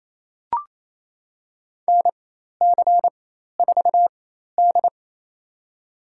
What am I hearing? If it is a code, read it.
Morse code, heard as a keyed tone, about 28 wpm, spaced as for 13 wpm, 700 Hz. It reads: NC4D
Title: USA Call Signs - 4 Characters - 4x Character Spacing - 28wpm